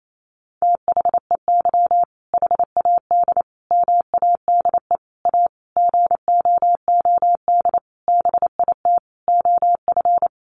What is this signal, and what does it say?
Morse code, heard as a keyed tone, about 28 wpm, 700 Hz. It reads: THEY HAD MADE A GOOD BIT OF
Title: Sentences from Top 900 Words 28wpm